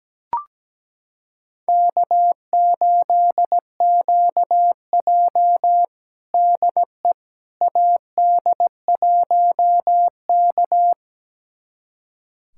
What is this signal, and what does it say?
Morse code, heard as a keyed tone, about 17 wpm, 700 Hz. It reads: K8QJ DE AD1K
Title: USA Callsign DE Callsign - 4 Characters - 17wpm